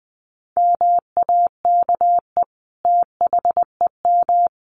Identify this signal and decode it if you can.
Morse code, heard as a keyed tone, about 20 wpm, 700 Hz. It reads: MAKE THEM
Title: Sentences from Top 900 Words 20wpm